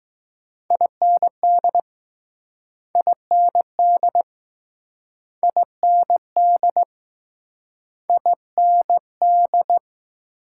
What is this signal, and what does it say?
Morse code, heard as a keyed tone, about 20 wpm, 700 Hz. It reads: IND IND IND IND